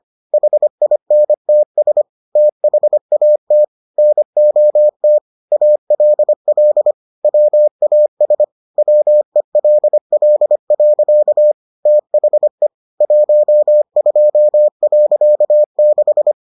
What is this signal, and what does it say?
Morse code, heard as a keyed tone, about 25 wpm, 600 Hz. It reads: HINTS THAT NOT ALL WAS WELL. THE 12.6